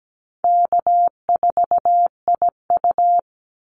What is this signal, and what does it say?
Morse code, heard as a keyed tone, about 17 wpm, 700 Hz. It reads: K4IU
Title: Easy Call Signs Repeated 3x - 17wpm